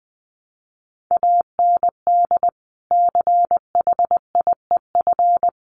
Morse code, heard as a keyed tone, about 19 wpm, 700 Hz. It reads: AND CHIEF
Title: Sets of 5 Words Encore2 20wpm